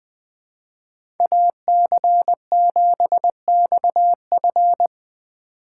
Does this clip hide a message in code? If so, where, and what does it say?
Yes: Morse code from 1.2 to 4.9 s, heard as a keyed tone, about 20 wpm, 700 Hz. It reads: AC7XF